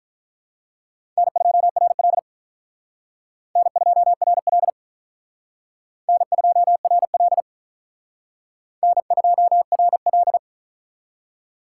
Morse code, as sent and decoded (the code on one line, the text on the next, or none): -. ..--- .-. .-.. / -. ..--- .-. .-.. / -. ..--- .-. .-.. / -. ..--- .-. .-..
N2RL N2RL N2RL N2RL